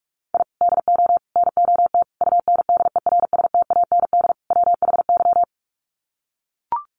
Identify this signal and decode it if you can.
Morse code, heard as a keyed tone, about 44 wpm, 700 Hz. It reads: I DO NOT UNDERSTAND WHY
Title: Sets of 5 Words Encore 45wpm